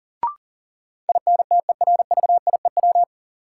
Morse code, heard as a keyed tone, about 40 wpm, 700 Hz. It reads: INTERVIEW
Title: Sentences from Top 1600 Words - New Word Drill Alt - 40wpm